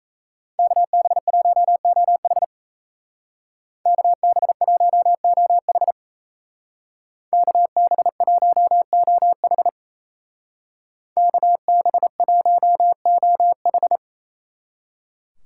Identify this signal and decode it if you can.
Morse code, heard as a keyed tone, about 34 wpm, 700 Hz. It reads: KB1OH KB1OH KB1OH KB1OH
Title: Speed Racing - USA 2x2 Call Signs 28wpm